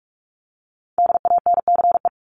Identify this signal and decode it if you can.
Morse code, heard as a keyed tone, about 44 wpm, 700 Hz. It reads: DANCE
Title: Top 501-750 Words 45wpm